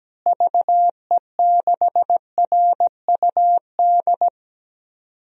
Morse code, heard as a keyed tone, about 17 wpm, 700 Hz. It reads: VE6RUD
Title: DXCC-100 Call Signs - 6 Characters - Repeated 3x - 17wpm